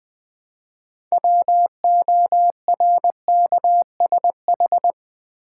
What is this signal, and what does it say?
Morse code, heard as a keyed tone, about 20 wpm, 700 Hz. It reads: WORKSH
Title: Build A Word - 3 Letters At A Time - 8-Letter Words - 20wpm